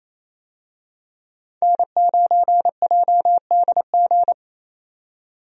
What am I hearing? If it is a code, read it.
Morse code, heard as a keyed tone, about 28 wpm, 700 Hz. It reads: N9JDG
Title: USA Call Signs - 5 Characters - 28wpm